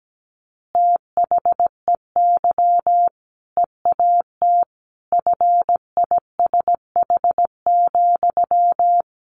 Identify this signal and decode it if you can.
Morse code, heard as a keyed tone, about 17 wpm, 700 Hz. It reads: THEY EAT FISH,
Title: Easy Phrases 17wpm